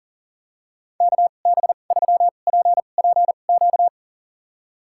Morse code, heard as a keyed tone, about 40 wpm, spaced as for 29 wpm, 700 Hz. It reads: KB3PPQ
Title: USA Call Signs - 6 Characters - 2x Character Spacing - 40wpm